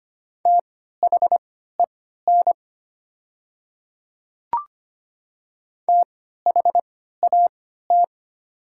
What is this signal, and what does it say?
Morse code, heard as a keyed tone, about 25 wpm, spaced as for 14 wpm, 700 Hz. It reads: THEN THAT
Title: 3x Character Spacing - Top 100 Words - 25wpm